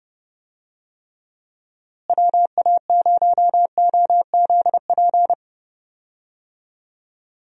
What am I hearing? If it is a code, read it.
Morse code, heard as a keyed tone, about 30 wpm, 700 Hz. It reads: WA0OZP
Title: USA Call Signs - 6 Characters - Repeated 2x - 30wpm